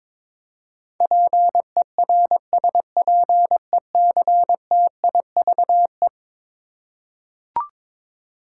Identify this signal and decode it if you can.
Morse code, heard as a keyed tone, about 22 wpm, 700 Hz. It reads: PERSPECTIVE
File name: practice / 22wpm / Eleven Letter Words - 22wpm